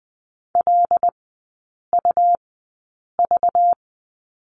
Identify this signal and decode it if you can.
Morse code, heard as a keyed tone, about 20 wpm, spaced as for 8 wpm, 700 Hz. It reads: LUV
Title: 191 Lesson 31e - Letter v - Three Characters 20wpm